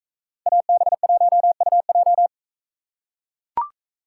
Morse code, heard as a keyed tone, about 43 wpm, 700 Hz. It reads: AB1UJ